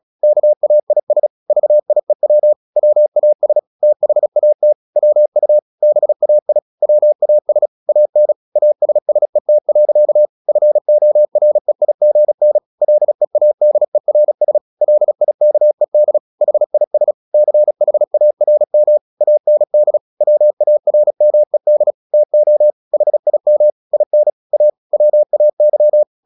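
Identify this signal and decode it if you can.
Morse code, heard as a keyed tone, about 36 wpm, 600 Hz. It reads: KAIS VIEW WAS THAT WU BAI WAS AN ASSET. FOREIGN LEADERS LIKED HIS CHARM AND WARMED TO HIM IN A WAY
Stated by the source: Morse practice at 36 wpm, text of Ken Follett's novel Never, from CW med Gnister.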